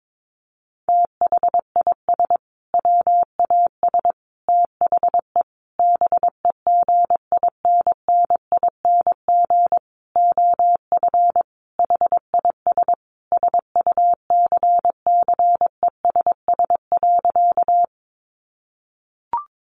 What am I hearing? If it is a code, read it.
Morse code, heard as a keyed tone, about 22 wpm, 700 Hz. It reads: THIS WAS THE BEGINNING OF HIS SUCCESS.
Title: Sentences from Top 900 Words 22wpm